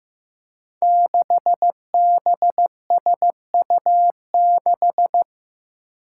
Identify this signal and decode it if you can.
Morse code, heard as a keyed tone, about 15 wpm, 700 Hz. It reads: 6BSU6
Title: Five Challenging Letter-Number Combinations 15wpm